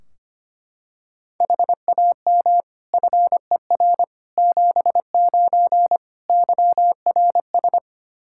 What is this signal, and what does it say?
Morse code, heard as a keyed tone, about 25 wpm, 700 Hz. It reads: HAM FER 79 YRS
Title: Practice QSO - Exchange 3 - More Info - 25wpm